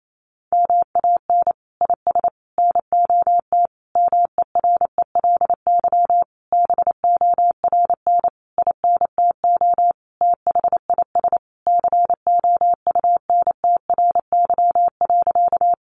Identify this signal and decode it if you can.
Morse code, heard as a keyed tone, about 28 wpm, 700 Hz. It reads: MAN IS NOT MERELY BORN INTO THIS COUNTRY.
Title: Sentences from Top 1400 Words 28wpm